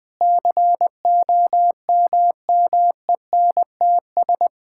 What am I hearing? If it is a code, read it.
Morse code, heard as a keyed tone, about 20 wpm, 700 Hz. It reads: COMMENTS
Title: Sets of 6 Words - 20wpm